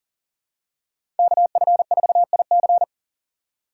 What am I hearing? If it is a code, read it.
Morse code, heard as a keyed tone, about 40 wpm, 700 Hz. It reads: KF4IC
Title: USA Call Signs - 5 Characters - 3x Character Spacing - 40wpm